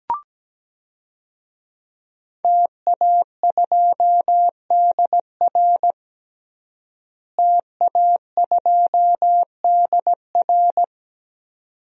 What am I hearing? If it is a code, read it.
Morse code, heard as a keyed tone, about 17 wpm, 700 Hz. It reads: TA2DR TA2DR
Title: DXCC-100 Call Signs - 5 Characters - Repeated 2x - 17wpm